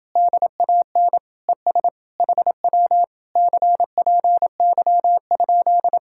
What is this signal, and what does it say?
Morse code, heard as a keyed tone, about 28 wpm, 700 Hz. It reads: DAN ES HW CPY?